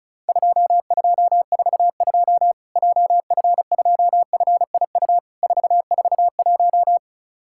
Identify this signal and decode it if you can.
Morse code, heard as a keyed tone, about 36 wpm, 700 Hz. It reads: 2242 JF2FIU 441
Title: Worked All Europe DX Contest - QTC Exchange - 35wpm